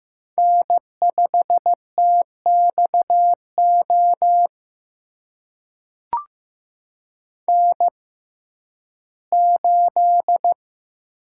Morse code, heard as a keyed tone, about 15 wpm, spaced as for 11 wpm, 700 Hz. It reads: N5TXO N8
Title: USA Call Signs - 5 Characters - 6x Character Spacing - 15wpm